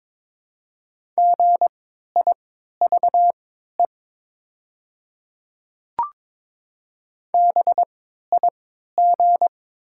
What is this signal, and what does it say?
Morse code, heard as a keyed tone, about 21 wpm, spaced as for 12 wpm, 700 Hz. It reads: GIVE BIG
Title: 3x Character Spacing - Top 200 Words - 22wpm